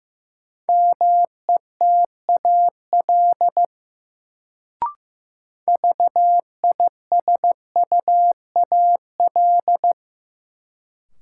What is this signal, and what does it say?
Morse code, heard as a keyed tone, about 15 wpm, 700 Hz. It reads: METAL VISUAL
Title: Sentences from Top 1000 Words - Review Part 6 - New Word Drill - 15wpm